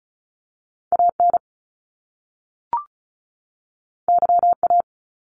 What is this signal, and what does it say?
Morse code, heard as a keyed tone, about 34 wpm, 700 Hz. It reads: AN YA
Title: Top Two Letter Combinations in Words 35wpm